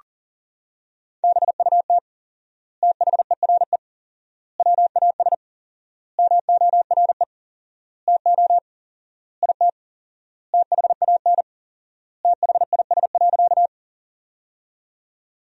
Morse code, heard as a keyed tone, about 40 wpm, 700 Hz. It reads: BUT THERE WAS MORE TO IT THAN THIS.
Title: 4x Word-Spacing - Sentences from Top 100 Words - 40wpm